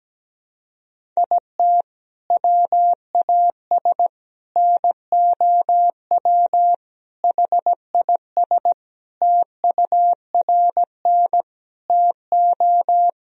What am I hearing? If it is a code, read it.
Morse code, heard as a keyed tone, about 17 wpm, 700 Hz. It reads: IT WAS NOW HIS TURN TO
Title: Sentences from Top 200 Words and Abbreviations 17wpm